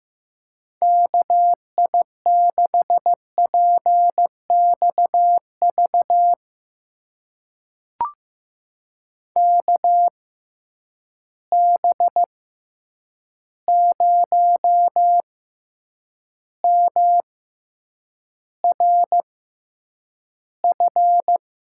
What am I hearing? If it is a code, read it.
Morse code, heard as a keyed tone, about 15 wpm, spaced as for 8 wpm, 700 Hz. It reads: KI6PXV KB0MRF
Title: USA Call Signs - 6 Characters - 6x Character Spacing - 15wpm